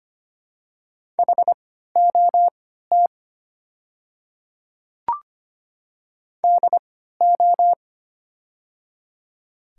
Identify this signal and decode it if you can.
Morse code, heard as a keyed tone, about 25 wpm, spaced as for 14 wpm, 700 Hz. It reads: HOT DO
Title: 3x Character Spacing - Top 100 Words - 25wpm